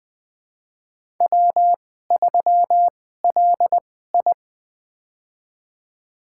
Morse code, heard as a keyed tone, about 20 wpm, spaced as for 15 wpm, 700 Hz. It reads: W3LI
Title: USA Call Signs - 4 Characters - 2x Character Spacing - 20wpm